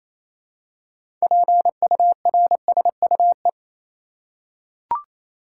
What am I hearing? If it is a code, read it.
Morse code, heard as a keyed tone, about 27 wpm, 700 Hz. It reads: PURSUE
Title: Six Letter Words - 28wpm